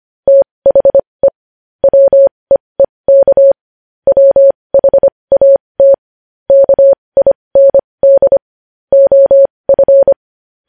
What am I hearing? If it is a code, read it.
Morse code, heard as a keyed tone, about 25 wpm, spaced as for 20 wpm, 550 Hz. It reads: THE WEEK WHAT KIND OF
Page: news headlines in morse code at 20 wpm